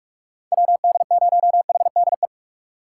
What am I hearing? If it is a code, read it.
Morse code, heard as a keyed tone, about 46 wpm, 700 Hz. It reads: WD0HDE
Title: USA Call Signs - 6 Characters - 4x Character Spacing - 45wpm